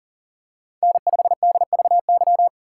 Morse code, heard as a keyed tone, about 40 wpm, 700 Hz. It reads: N5DVY
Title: USA Call Signs - 5 Characters - 4x Character Spacing - 40wpm